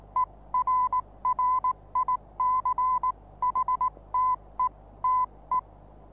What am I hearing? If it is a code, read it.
Morse code, heard as a keyed tone, about 19 wpm, spaced as for 16 wpm, 1010 Hz. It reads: ERRICHTETE
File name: repos / Morsekurs / DJ4UF Morsen26